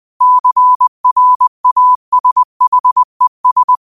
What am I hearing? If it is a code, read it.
Morse code, heard as a keyed tone, about 20 wpm, 1000 Hz. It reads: CRASHES